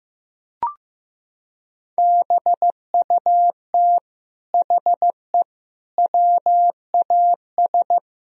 Morse code, heard as a keyed tone, about 15 wpm, 700 Hz. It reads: BUT HE WAS
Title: Sentences from Top 1000 Words - Review Part 3 - 15wpm